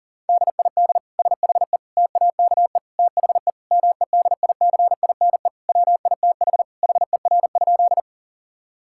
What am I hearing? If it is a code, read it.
Morse code, heard as a keyed tone, about 40 wpm, 700 Hz. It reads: DID SHE TAKE THE MEDICINE WITH HER?
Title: Sentences from Top 1000 Words - Review Part 6 - 40wpm